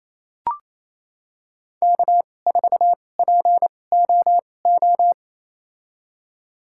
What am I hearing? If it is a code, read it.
Morse code, heard as a keyed tone, about 28 wpm, spaced as for 20 wpm, 700 Hz. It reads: K4POO
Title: USA Call Signs - 5 Characters - 2x Character Spacing - 28wpm